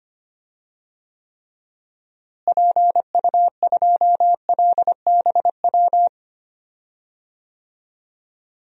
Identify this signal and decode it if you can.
Morse code, heard as a keyed tone, about 25 wpm, 700 Hz. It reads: PU2LBW
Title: DXCC-100 Call Signs - 6 Characters - Repeated 3x - 25wpm